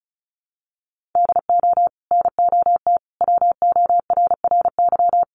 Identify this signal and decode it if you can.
Morse code, heard as a keyed tone, about 34 wpm, 700 Hz. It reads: DO NOT WORRY
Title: Sets of 5 Words 35wpm